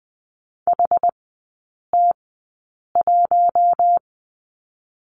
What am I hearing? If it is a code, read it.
Morse code, heard as a keyed tone, about 20 wpm, 700 Hz. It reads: H T 1
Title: 121 Lesson 20e - Number 3 - Three Characters 20wpm